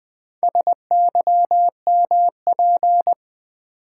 Morse code, heard as a keyed tone, about 20 wpm, 700 Hz. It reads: SYMP
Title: Build A Word - 2 Letters At A Time - 8-Letter Words - 20wpm